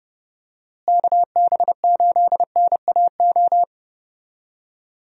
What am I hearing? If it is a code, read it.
Morse code, heard as a keyed tone, about 30 wpm, 700 Hz. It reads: KB8NAO
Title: USA Call Signs - 6 Characters - Repeated 2x - 30wpm